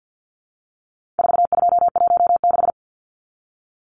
Morse code, heard as a keyed tone, about 50 wpm, 700 Hz. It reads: 4216
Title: Numbers - 4-Digit 50wpm